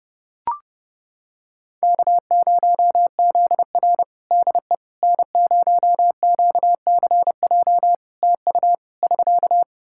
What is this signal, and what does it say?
Morse code, heard as a keyed tone, about 32 wpm, 700 Hz. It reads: K0ZR DE N0QCJ TU <SK>
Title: Practice QSO - Exchange 4 - Ending - 30wpm